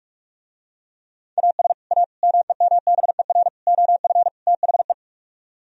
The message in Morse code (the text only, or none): AS A MEMBER OF THE